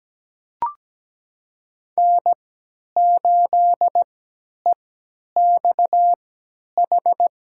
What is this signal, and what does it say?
Morse code, heard as a keyed tone, about 17 wpm, spaced as for 10 wpm, 700 Hz. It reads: N8EXH
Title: USA Call Signs - 3x Character Spacing - 17wpm